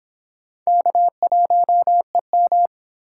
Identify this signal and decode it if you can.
Morse code, heard as a keyed tone, about 23 wpm, 700 Hz. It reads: K1EM